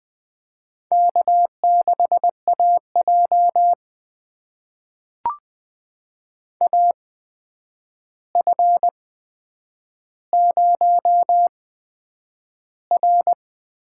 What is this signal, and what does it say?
Morse code, heard as a keyed tone, about 20 wpm, spaced as for 9 wpm, 700 Hz. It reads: K6AJ AF0R
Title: USA Call Signs - 4 Characters - 8x Character Spacing - 20wpm